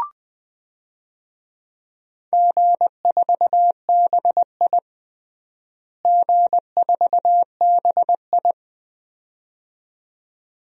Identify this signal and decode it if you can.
Morse code, heard as a keyed tone, about 20 wpm, 700 Hz. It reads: G4BI G4BI
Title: DXCC-100 Call Signs - 4 Characters - Repeated 2x - 20wpm